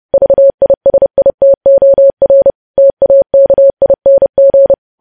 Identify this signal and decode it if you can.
Morse code, heard as a keyed tone, about 30 wpm, 550 Hz. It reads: VISITOR TAKING
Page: news headlines in morse code at 30 wpm